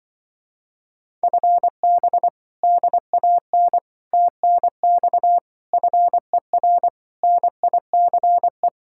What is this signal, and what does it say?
Morse code, heard as a keyed tone, about 24 wpm, 700 Hz. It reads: FB DAN TNX FER NICE
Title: Speed-Racing Common QSO Phrases - 20wpm